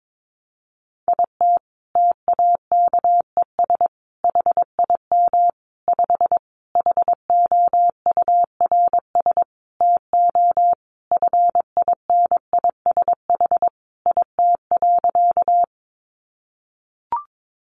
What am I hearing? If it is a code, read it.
Morse code, heard as a keyed tone, about 22 wpm, 700 Hz. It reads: IT TAKES HIM 5 HOURS TO FINISH IT.
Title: Additional Easy Phrases 22wpm